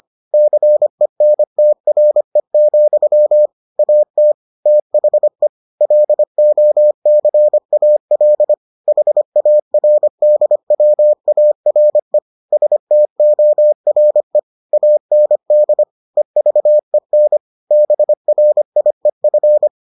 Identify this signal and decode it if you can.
Morse code, heard as a keyed tone, about 25 wpm, 600 Hz. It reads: CENTRE, AT THE LOCAL HARDWARE STORE AND EVEN BRIEF